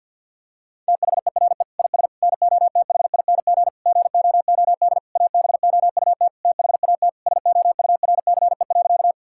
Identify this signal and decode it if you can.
Morse code, heard as a keyed tone, about 50 wpm, 700 Hz. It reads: THERE IS NOTHING GOOD ABOUT THAT SOURCE.
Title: Sentences from Top 1000 Words - Review Part 5 - 50wpm